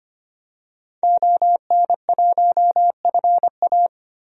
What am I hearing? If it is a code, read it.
Morse code, heard as a keyed tone, about 25 wpm, 700 Hz. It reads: ON1FA
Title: DXCC-100 Call Signs - 5 Characters - Repeated 2x - 25wpm